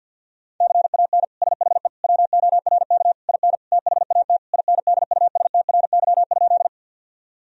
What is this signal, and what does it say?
Morse code, heard as a keyed tone, about 51 wpm, 700 Hz. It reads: CAN SHE WORK IN THAT INDUSTRY?
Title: Sentences from Top 1000 Words - Review Part 4 - 50wpm